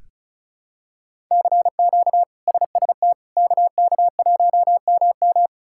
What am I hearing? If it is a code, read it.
Morse code, heard as a keyed tone, about 34 wpm, 700 Hz. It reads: CQ SST KK1MM
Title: K1USN SST QSO Practice - 35wpm